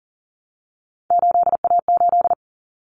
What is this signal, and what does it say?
Morse code, heard as a keyed tone, about 40 wpm, 700 Hz. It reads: 8A8
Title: Common Cut Numbers - 3-Digit 40wpm